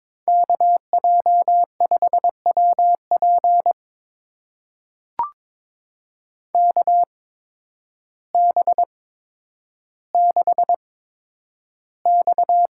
Morse code, heard as a keyed tone, about 22 wpm, spaced as for 10 wpm, 700 Hz. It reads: KJ5WP KB6X
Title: USA Call Signs - 5 Characters - 8x Character Spacing - 22wpm